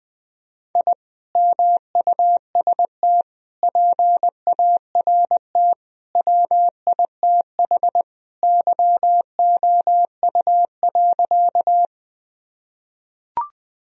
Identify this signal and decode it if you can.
Morse code, heard as a keyed tone, about 20 wpm, 700 Hz. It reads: I MUST PART WITH YOU.
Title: Sentences from Top 1000 Words - Review Part 1 - 20wpm